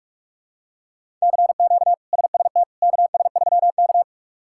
Morse code, heard as a keyed tone, about 46 wpm, 700 Hz. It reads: CQ SST KS3K
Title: K1USN SST QSO Practice - 45wpm